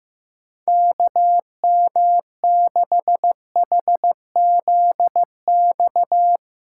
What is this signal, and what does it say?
Morse code, heard as a keyed tone, about 16 wpm, 700 Hz. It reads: KM6HZX
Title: USA Callsign DE Callsign 15wpm